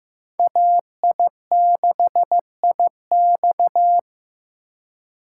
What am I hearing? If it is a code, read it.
Morse code, heard as a keyed tone, about 18 wpm, 700 Hz. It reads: AI6IX
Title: Speed Racing - USA 2x2 Call Signs 15wpm